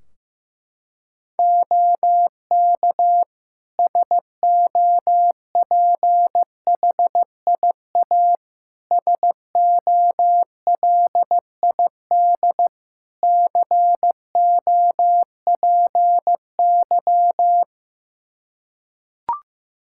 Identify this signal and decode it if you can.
Morse code, heard as a keyed tone, about 15 wpm, 700 Hz. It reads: OK SOPHIA SOLID COPY
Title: Practice QSO - Exchange 3 - More Info - 15wpm